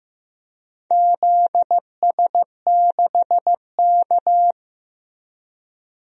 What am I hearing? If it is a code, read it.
Morse code, heard as a keyed tone, about 15 wpm, 700 Hz. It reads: ZS6K
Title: DXCC-100 Call Signs - 4 Characters - Repeated 2x - 15wpm